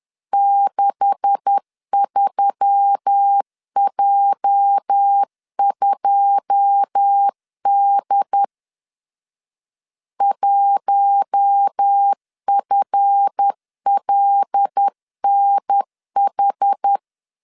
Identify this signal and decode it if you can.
Morse code, heard as a keyed tone, about 10 wpm, 800 Hz. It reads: 63J2D 1FLNH
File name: LEZIONI / 10 SIGNS / 10D COMA 10wpm 3013